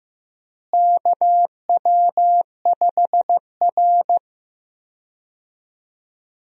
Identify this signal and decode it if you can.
Morse code, heard as a keyed tone, about 15 wpm, 700 Hz. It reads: KW5R
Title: USA Call Signs - 4 Characters - Repeated 3x - 15wpm